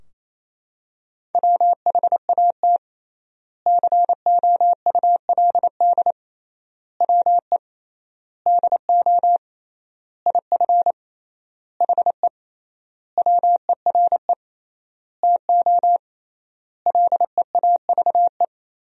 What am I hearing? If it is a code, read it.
Morse code, heard as a keyed tone, about 28 wpm, 700 Hz. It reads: WHAT COULD WE DO IF HE WERE TO LEAVE
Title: 3x Word-Spacing - Sentences from Top 400 Words - 28wpm